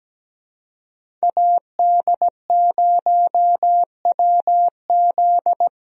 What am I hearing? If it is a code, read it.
Morse code, heard as a keyed tone, about 17 wpm, 700 Hz. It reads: AD0WZ
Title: USA Call Signs - 5 Characters - Repeated 2x - 17wpm